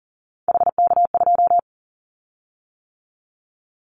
Morse code, heard as a keyed tone, about 40 wpm, 700 Hz. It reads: HK2
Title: Callsign Prefixes for Top 100 DXCC - 40wpm